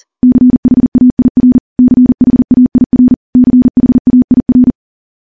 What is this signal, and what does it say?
Morse code, heard as a keyed tone, about 40 wpm, 250 Hz. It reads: CHAIR CHAIR CHAIR
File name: morse / morse-training-40WPM